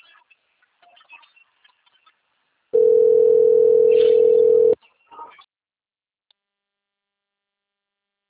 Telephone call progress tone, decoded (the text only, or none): ringback tone